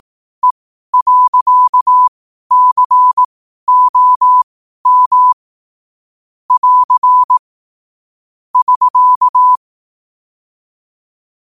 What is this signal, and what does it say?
Morse code, heard as a keyed tone, about 18 wpm, spaced as for 12 wpm, 1000 Hz. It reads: E.COM <AR> <SK>
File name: qotdfiles / 12wpm